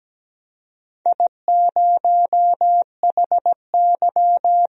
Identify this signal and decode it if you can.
Morse code, heard as a keyed tone, about 17 wpm, 700 Hz. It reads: I0HY
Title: DXCC-100 Call Signs - 4 Characters - 17wpm